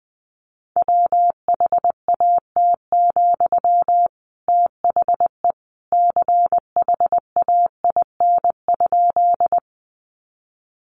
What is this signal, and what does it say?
Morse code, heard as a keyed tone, about 20 wpm, 700 Hz. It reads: WHAT, THE CHAIN?
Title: Sentences from Top 1500 Words 20wpm